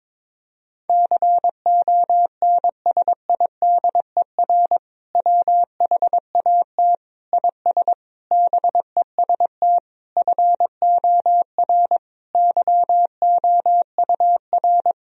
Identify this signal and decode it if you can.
Morse code, heard as a keyed tone, about 22 wpm, 700 Hz. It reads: CONSIDER WHAT IS BEST FOR YOUR
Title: Sentences from Top 1000 Words - Review Part 4 - 22wpm